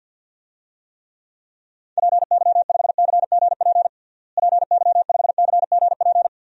Morse code, heard as a keyed tone, about 51 wpm, 700 Hz. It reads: PY5CGP PY5CGP
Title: DXCC-100 Call Signs - 6 Characters - Repeated 3x - 50wpm